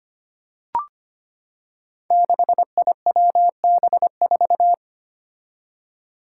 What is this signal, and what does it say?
Morse code, heard as a keyed tone, about 26 wpm, 700 Hz. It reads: 6IWB4
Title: Five Challenging Letter-Number Combinations 25wpm